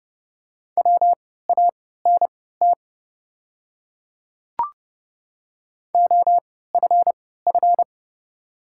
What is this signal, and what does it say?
Morse code, heard as a keyed tone, about 29 wpm, spaced as for 17 wpm, 700 Hz. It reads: WANT OFF
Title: 3x Character Spacing - Top 200 Words - 30wpm